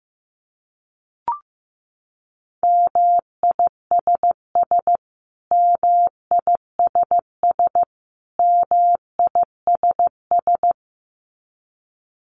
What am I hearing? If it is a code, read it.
Morse code, heard as a keyed tone, about 15 wpm, 700 Hz. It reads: MISS MISS MISS
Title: Top 500 Words Repeated 15wpm